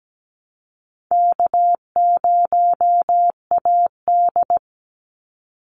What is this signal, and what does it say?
Morse code, heard as a keyed tone, about 17 wpm, 700 Hz. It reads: K0AD
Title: Call Signs Repeated 3x - 17wpm